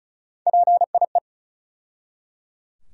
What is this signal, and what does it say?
Morse code, heard as a keyed tone, about 34 wpm, 700 Hz. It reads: PIE
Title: USA Call Sign Postfixes - 3 Letters 35wpm